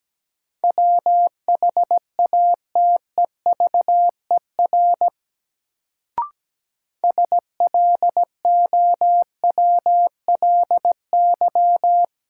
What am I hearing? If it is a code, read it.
Morse code, heard as a keyed tone, about 17 wpm, 700 Hz. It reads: WHATEVER SLOWLY
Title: Sentences from Top 1100 Words - New Word Drill Alt - 17wpm